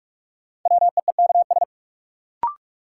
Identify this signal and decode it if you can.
Morse code, heard as a keyed tone, about 42 wpm, 700 Hz. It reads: WEEKS
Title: Sets of 5 Words - New Word Drill - 45wpm